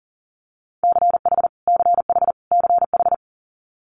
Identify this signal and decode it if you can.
Morse code, heard as a keyed tone, about 37 wpm, 700 Hz. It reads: CH CH CH
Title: Multi-letter Phonemes Repeated 3x 40wpm